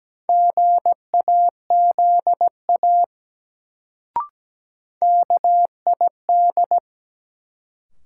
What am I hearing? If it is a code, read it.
Morse code, heard as a keyed tone, about 17 wpm, 700 Hz. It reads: GAZA KID
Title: Sets of 5 Words - New Word Drill - 17wpm